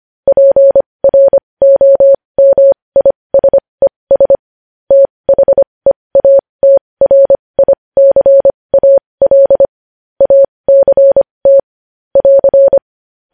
Morse code, heard as a keyed tone, about 25 wpm, spaced as for 20 wpm, 550 Hz. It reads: PROMISES THEATRICAL ACT <AR>
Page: news-headlines-in-morse-code-at-20-wpm